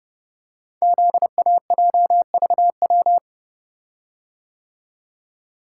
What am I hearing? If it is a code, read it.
Morse code, heard as a keyed tone, about 30 wpm, 700 Hz. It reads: ZAJVW
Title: Code Groups - 5 Letters - 30wpm